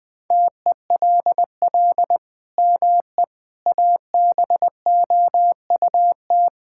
Morse code, heard as a keyed tone, about 20 wpm, 700 Hz. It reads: TELL ME ABOUT
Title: Sentences from Top 1000 Words - Review Part 5 - 20wpm